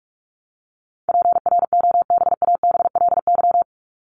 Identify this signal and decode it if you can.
Morse code, heard as a keyed tone, about 44 wpm, 700 Hz. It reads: PROBABLY
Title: Top 501-750 Words 45wpm